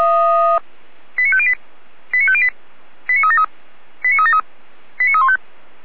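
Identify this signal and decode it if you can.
Five-tone selective call (ZVEI), heard as quick runs of five tones, tones of about 70 ms. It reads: 89508 89508 89373 89373 89315
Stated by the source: ZVEI five-tone selective call 89068 received by SDR.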